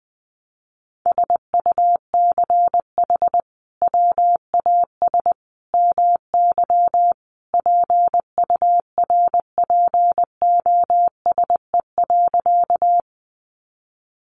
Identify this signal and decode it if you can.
Morse code, heard as a keyed tone, about 20 wpm, 700 Hz. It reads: SUCH WAS MY PURPOSE.
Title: Sentences from Top 800 Words 20wpm